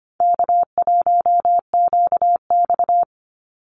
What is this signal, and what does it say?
Morse code, heard as a keyed tone, about 25 wpm, 700 Hz. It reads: K1QX
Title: Call Signs Repeated 2x - 25wpm